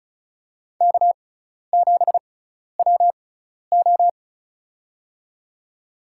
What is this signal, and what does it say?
Morse code, heard as a keyed tone, about 34 wpm, spaced as for 12 wpm, 700 Hz. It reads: K7WO
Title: USA Call Signs - 6x Character Spacing - 35wpm